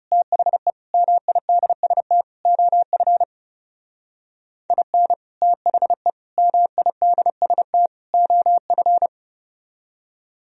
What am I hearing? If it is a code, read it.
Morse code, heard as a keyed tone, about 35 wpm, 700 Hz. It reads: THE MIDST OF IN THE MIDST OF
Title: Speed-Racing - Sets of 4 Words 30wpm